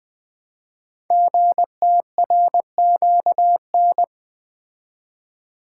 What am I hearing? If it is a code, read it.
Morse code, heard as a keyed tone, about 20 wpm, 700 Hz. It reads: GTRQN
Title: Code Groups - 5 Letters - 20wpm